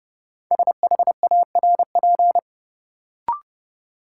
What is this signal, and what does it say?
Morse code, heard as a keyed tone, about 30 wpm, 700 Hz. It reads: SHARP
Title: Sentences from Top 1600 Words - New Word Drill Alt - 30wpm